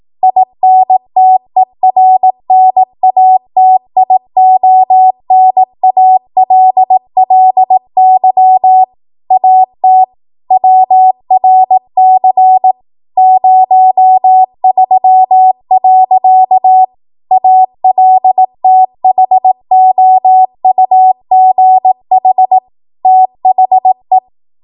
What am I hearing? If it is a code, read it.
Morse code, heard as a keyed tone, about 18 wpm, 750 Hz. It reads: INTERNATIONALLY AT WRC 03. ALTHOUGH THE
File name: Morse / 18WPM